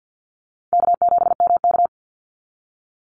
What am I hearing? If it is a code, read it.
Morse code, heard as a keyed tone, about 50 wpm, 700 Hz. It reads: K7MK
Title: NA SOTA Activator Callsigns 50wpm